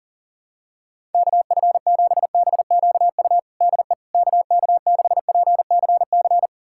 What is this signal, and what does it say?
Morse code, heard as a keyed tone, about 40 wpm, 700 Hz. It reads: KF7BQU DE KK6PCC
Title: USA Callsign DE Callsign 40wpm